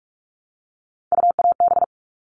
Morse code, heard as a keyed tone, about 44 wpm, 700 Hz. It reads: UAB
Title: All Cut Numbers - 3-Digit 45wpm